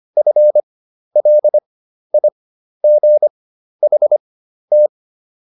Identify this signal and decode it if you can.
Morse code, heard as a keyed tone, about 25 wpm, spaced as for 12 wpm, 600 Hz. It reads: FLIGHT